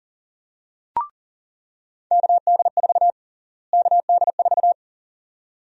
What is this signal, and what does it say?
Morse code, heard as a keyed tone, about 40 wpm, 700 Hz. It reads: KD4 KD4
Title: USA Call Sign Prefixes - Repeated 2x - 40wpm